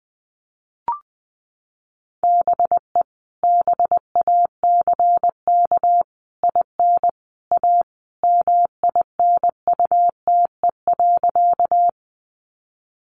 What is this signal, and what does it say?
Morse code, heard as a keyed tone, about 20 wpm, 700 Hz. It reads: BE BACK IN A MINUTE.
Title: Sentences from Top 500 Words 20wpm